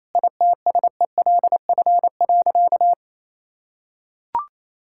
Morse code, heard as a keyed tone, about 28 wpm, 700 Hz. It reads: ITSELF.
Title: Sentences from Top 1000 Words - Review Part 5 - 28wpm